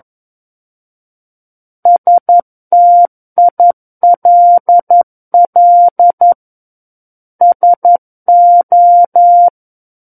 Morse code, heard as a keyed tone, about 11 wpm, 700 Hz. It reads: STILL SO
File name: Book One / 02-Chapter 2 - 11 WPM